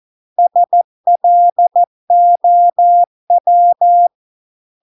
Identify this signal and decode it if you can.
Morse code, heard as a keyed tone, about 14 wpm, 700 Hz. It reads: SLOW